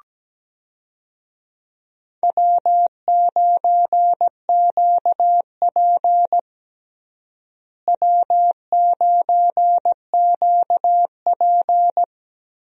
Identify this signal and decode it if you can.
Morse code, heard as a keyed tone, about 17 wpm, 700 Hz. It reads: W9QP W9QP
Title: USA Call Signs - 4 Characters - Repeated 3x - 17wpm